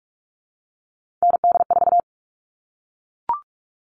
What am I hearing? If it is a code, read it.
Morse code, heard as a keyed tone, about 44 wpm, 700 Hz. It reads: ND4
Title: All Cut Numbers - 3-Digit 45wpm